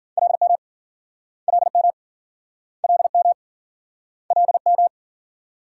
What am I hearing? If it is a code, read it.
Morse code, heard as a keyed tone, about 53 wpm, 700 Hz. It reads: LM LM LM LM